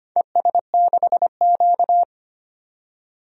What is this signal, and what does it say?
Morse code, heard as a keyed tone, about 25 wpm, 700 Hz. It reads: ES6Q
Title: DXCC-100 Call Signs - 4 Characters - 25wpm